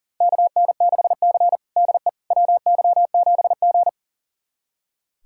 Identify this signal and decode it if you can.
Morse code, heard as a keyed tone, about 40 wpm, 700 Hz. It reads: KN6C DE WY7G
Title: USA Callsign DE Callsign - 4 Characters - 40wpm